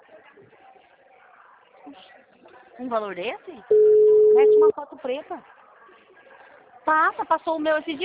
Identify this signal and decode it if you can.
Ringback tone (Europe, 425 Hz), with speech.